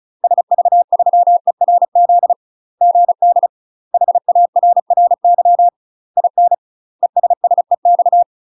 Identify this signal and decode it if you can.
Morse code, heard as a keyed tone, about 34 wpm, 700 Hz. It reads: SV3ERZ GD HARRY IN ESSEX
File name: UK MASTER QSOs TRAINING / 35WPM / UK MASTER QSOs 0004